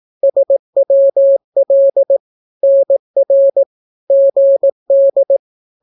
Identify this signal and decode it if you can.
Morse code, heard as a keyed tone, about 18 wpm, 550 Hz. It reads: SWL NR GD
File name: naukapodstaw / skroty / skroty